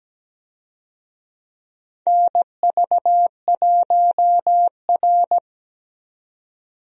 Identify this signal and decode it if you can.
Morse code, heard as a keyed tone, about 17 wpm, 700 Hz. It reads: NV1R
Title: USA Call Signs - 4 Characters - 17wpm